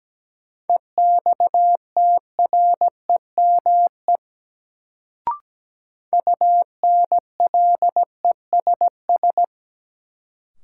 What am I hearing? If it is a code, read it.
Morse code, heard as a keyed tone, about 17 wpm, 700 Hz. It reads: EXTREME UNLESS
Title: Sentences from Top 1300 Words - New Word Drill Alt - 17wpm